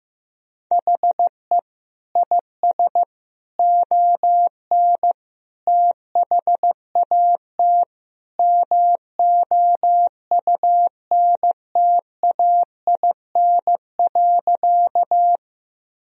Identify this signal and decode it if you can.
Morse code, heard as a keyed tone, about 15 wpm, 700 Hz. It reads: HE IS ON THAT MOUNTAIN.
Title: Sentences from Top 1000 Words - Review Part 1 - 15wpm